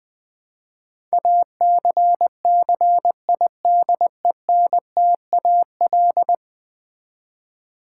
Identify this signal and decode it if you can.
Morse code, heard as a keyed tone, about 20 wpm, 700 Hz. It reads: ACCIDENTAL
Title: Build A Word - 5 Letters At A Time - 12-Letter Words - 20wpm